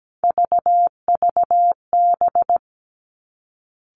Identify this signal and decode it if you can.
Morse code, heard as a keyed tone, about 17 wpm, 700 Hz. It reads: VVB
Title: Three Challenging Letter-Number Combinations 17wpm